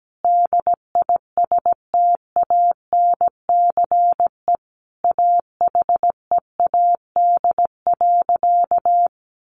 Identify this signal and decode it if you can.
Morse code, heard as a keyed tone, about 17 wpm, 700 Hz. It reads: DISTANCE AHEAD.
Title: Sentences from Top 1400 Words 17wpm